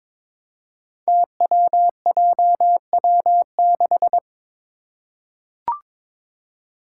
Morse code, heard as a keyed tone, about 22 wpm, 700 Hz. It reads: TWJW6
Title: Five Challenging Letter-Number Combinations 22wpm